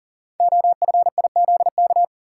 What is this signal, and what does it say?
Morse code, heard as a keyed tone, about 42 wpm, 690 Hz. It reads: OFIZK
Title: Code Groups - 5 Letters - 40wpm